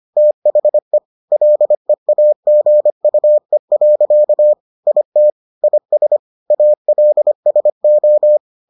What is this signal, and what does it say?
Morse code, heard as a keyed tone, about 25 wpm, 600 Hz. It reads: THE LEAGUE. IT IS ALSO